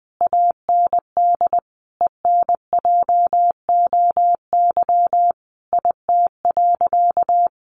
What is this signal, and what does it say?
Morse code, heard as a keyed tone, about 20 wpm, 700 Hz. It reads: AND ENJOY IT.
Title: Sentences from Top 1400 Words 20wpm